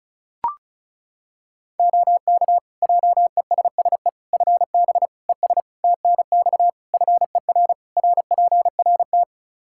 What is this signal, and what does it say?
Morse code, heard as a keyed tone, about 34 wpm, 700 Hz. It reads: OK JESSE FB ES TNX FER RPRT
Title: Practice QSO - Exchange 2 - Info - 35wpm